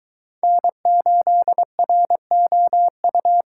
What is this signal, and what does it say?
Morse code, heard as a keyed tone, about 23 wpm, 700 Hz. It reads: N8ROU